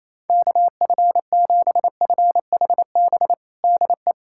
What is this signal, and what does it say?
Morse code, heard as a keyed tone, about 28 wpm, 700 Hz. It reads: KF7FHB DE
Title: USA Callsign DE Callsign 28wpm